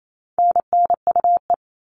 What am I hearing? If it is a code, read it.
Morse code, heard as a keyed tone, about 28 wpm, 700 Hz. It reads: NNUE
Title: All Cut Numbers - 4-Digit 28wpm